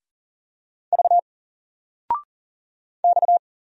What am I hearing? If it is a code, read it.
Morse code, heard as a keyed tone, about 40 wpm, 700 Hz. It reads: V X
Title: Single Letter-Number Alt 40wpm